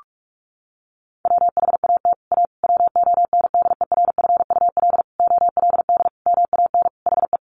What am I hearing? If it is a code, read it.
Morse code, heard as a keyed tone, about 44 wpm, 700 Hz. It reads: WHAT A WONDERFUL OLD MAN HE
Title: Sentences from Top 1100 Words 45wpm